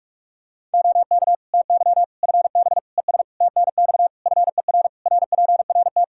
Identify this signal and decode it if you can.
Morse code, heard as a keyed tone, about 41 wpm, 700 Hz. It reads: OK TY FB ES TNX FER RPRT
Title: K1USN SST QSO Practice 45wpm